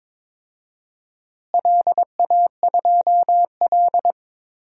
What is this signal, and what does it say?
Morse code, heard as a keyed tone, about 22 wpm, 700 Hz. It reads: LA2L
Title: DXCC-100 Call Signs - 4 Characters - Repeated 2x - 22wpm